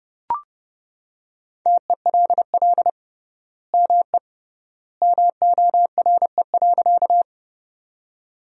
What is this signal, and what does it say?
Morse code, heard as a keyed tone, about 30 wpm, 700 Hz. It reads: TELL ME MORE.
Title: 3x Word-Spacing - Sentences from Top 300 Words - 30wpm